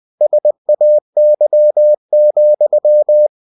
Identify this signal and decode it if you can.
Morse code, heard as a keyed tone, about 20 wpm, 600 Hz. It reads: SAY,
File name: cw-books / winnie-the-pooh / winnie-20wpm-page-0061